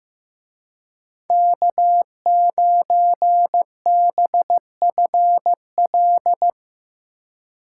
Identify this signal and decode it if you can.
Morse code, heard as a keyed tone, about 16 wpm, 700 Hz. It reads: K9BFL
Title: Speed Racing - USA 1x3 Call Signs 15wpm